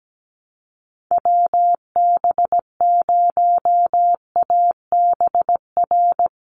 Morse code, heard as a keyed tone, about 17 wpm, 700 Hz. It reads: WB0ABR
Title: USA Call Signs Repeated 3x 17wpm